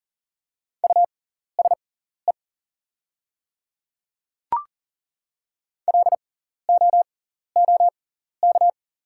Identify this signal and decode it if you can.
Morse code, heard as a keyed tone, about 39 wpm, spaced as for 14 wpm, 700 Hz. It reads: USE LOOK
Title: 6x Character Spacing - Top 100 Words - 40wpm